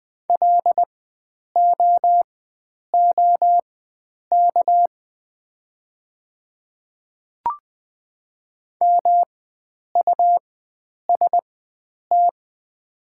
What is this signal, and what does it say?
Morse code, heard as a keyed tone, about 20 wpm, spaced as for 9 wpm, 700 Hz. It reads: LOOK MUST
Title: 4x Character Spacing - Top 300 Words - 20wpm